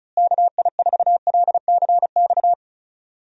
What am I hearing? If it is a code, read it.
Morse code, heard as a keyed tone, about 36 wpm, 700 Hz. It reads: KI4LCX
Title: USA Call Signs - 6 Characters - Repeated 2x - 35wpm